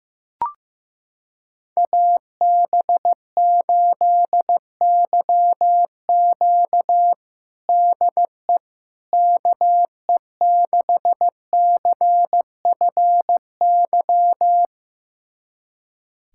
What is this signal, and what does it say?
Morse code, heard as a keyed tone, about 15 wpm, 700 Hz. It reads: AB8YQ DE KE6CFY
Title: USA Callsign DE Callsign 15wpm